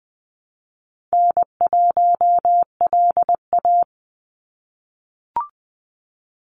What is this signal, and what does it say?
Morse code, heard as a keyed tone, about 20 wpm, 700 Hz. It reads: N1LA
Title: USA Call Signs 20wpm